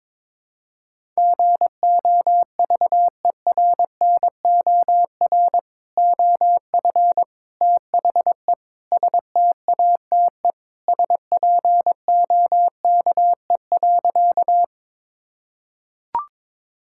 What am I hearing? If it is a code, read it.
Morse code, heard as a keyed tone, about 22 wpm, 700 Hz. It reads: GOVERNOR OF THE STATE SPOKE.
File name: practice / 22wpm / Sentences with 3-Word Phrases - 22wpm